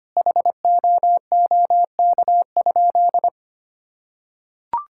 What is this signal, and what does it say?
Morse code, heard as a keyed tone, about 25 wpm, 700 Hz. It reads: HOOK?
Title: Sentences from Top 1600 Words 25wpm